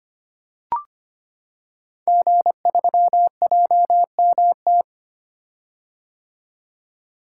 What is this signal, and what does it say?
Morse code, heard as a keyed tone, about 25 wpm, 700 Hz. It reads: G3JMT
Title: Code Groups - 5 Letter:Numbers - 25wpm